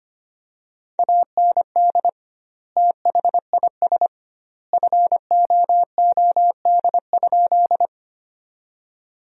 Morse code, heard as a keyed tone, about 25 wpm, 700 Hz. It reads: AND THIS FOOD?
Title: 2x Word-Spacing - Sentences from Top 300 Words - 25wpm